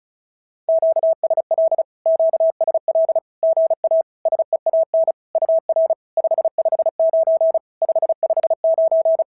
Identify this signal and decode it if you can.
Morse code, heard as a keyed tone, about 36 wpm, 690 Hz. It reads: QSL QSL GA SEAN UR 559 559
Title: SOTA QSO Practice 35wpm